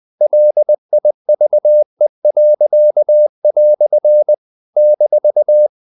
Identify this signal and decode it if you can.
Morse code, heard as a keyed tone, about 20 wpm, 600 Hz. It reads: LIVE." -
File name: CW / Practice / EnglishQuotations0320